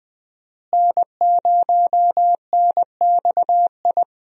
Morse code, heard as a keyed tone, about 20 wpm, 700 Hz. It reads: N0NXI
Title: USA Call Signs - 5 Characters - Repeated 3x - 20wpm